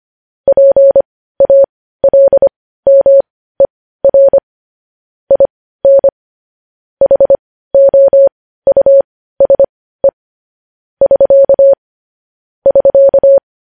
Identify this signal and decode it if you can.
Morse code, heard as a keyed tone, about 25 wpm, spaced as for 15 wpm, 550 Hz. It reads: PALMER IN HOUSE <SK> <SK>
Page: news-headlines-in-morse-code-at-15-wpm